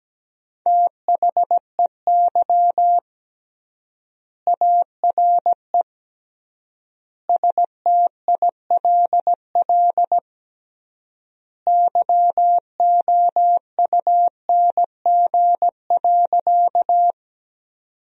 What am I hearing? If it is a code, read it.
Morse code, heard as a keyed tone, about 17 wpm, 700 Hz. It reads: THEY ARE STILL YOUNG.
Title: 3x Word-Spacing - Sentences from Top 400 Words - 17wpm